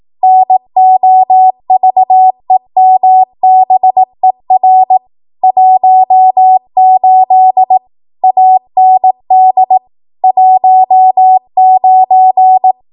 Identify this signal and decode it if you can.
Morse code, heard as a keyed tone, about 18 wpm, 750 Hz. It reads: NOVEMBER 18 AND 19
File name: Archive / 18 WPM / 18WPM